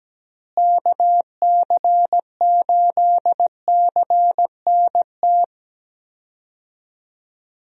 Morse code, heard as a keyed tone, about 17 wpm, 700 Hz. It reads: KC8CNT
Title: USA Call Signs - 6 Characters - Repeated 2x - 17wpm